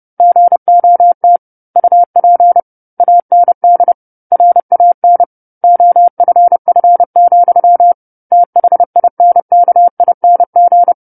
Morse code, heard as a keyed tone, about 30 wpm, 700 Hz. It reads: GOT UP AND RAN OFF, THINKING
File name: 10-Chapter-10 30 WPM